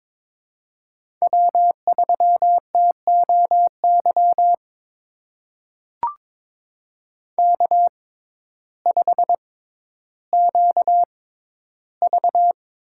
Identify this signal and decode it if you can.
Morse code, heard as a keyed tone, about 22 wpm, spaced as for 12 wpm, 700 Hz. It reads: W3TOY K5QV
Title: USA Call Signs - 6x Character Spacing - 22wpm